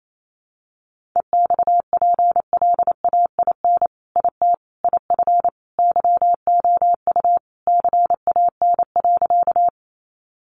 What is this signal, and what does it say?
Morse code, heard as a keyed tone, about 28 wpm, 700 Hz. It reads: EXPLAIN IT IF YOU CAN.
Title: Sentences from Top 1100 Words 28wpm